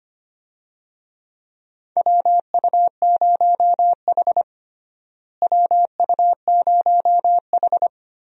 Morse code, heard as a keyed tone, about 25 wpm, 700 Hz. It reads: WU0H WU0H
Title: USA Call Signs - 4 Characters - Repeated 2x - 25wpm